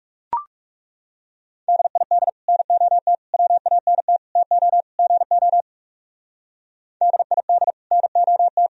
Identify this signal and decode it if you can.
Morse code, heard as a keyed tone, about 38 wpm, 700 Hz. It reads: DID NOT WANT TO GO DID NOT
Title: Speed-Racing - Sets of 5 Words 30wpm